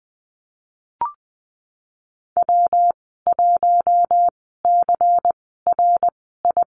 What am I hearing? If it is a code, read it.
Morse code, heard as a keyed tone, about 20 wpm, spaced as for 15 wpm, 700 Hz. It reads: W1CRI
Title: 173 Lesson 28g - Number 9 - Call Signs 20wpm